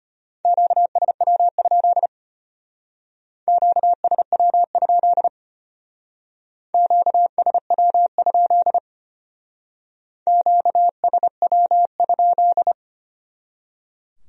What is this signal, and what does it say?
Morse code, heard as a keyed tone, about 32 wpm, 700 Hz. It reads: QSW? QSW? QSW? QSW?